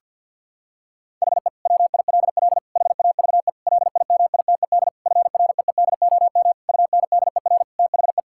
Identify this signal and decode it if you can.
Morse code, heard as a keyed tone, about 52 wpm, 700 Hz. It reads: HE WILL HAVE LIMITED FREEDOM UNDER THE
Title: Sentences from Top 1000 Words - Review Part 6 - 50wpm